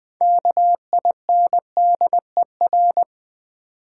Morse code, heard as a keyed tone, about 20 wpm, 700 Hz. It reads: KINDER
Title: Build A Word - 6 Letters At A Time - 12-Letter Words - 20wpm